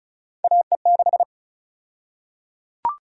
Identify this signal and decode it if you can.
Morse code, heard as a keyed tone, about 34 wpm, 700 Hz. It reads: AE6
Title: USA Call Sign Prefixes - 35wpm